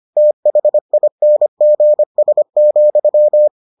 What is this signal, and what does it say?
Morse code, heard as a keyed tone, about 25 wpm, 600 Hz. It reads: THINGS,